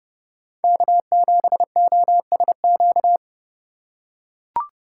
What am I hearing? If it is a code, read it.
Morse code, heard as a keyed tone, about 31 wpm, 700 Hz. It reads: K7OSQ
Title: USA Call Signs - 5 Characters - 4x Character Spacing - 30wpm